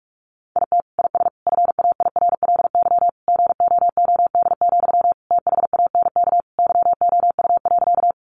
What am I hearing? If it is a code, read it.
Morse code, heard as a keyed tone, about 44 wpm, 700 Hz. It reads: IT IS FAIRLY GOOD, THANK YOU.
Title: Sentences from Top 1100 Words 45wpm